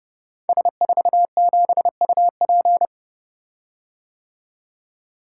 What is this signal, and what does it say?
Morse code, heard as a keyed tone, about 28 wpm, 700 Hz. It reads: S47UP
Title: Code Groups - 5 Letter:Numbers - 30wpm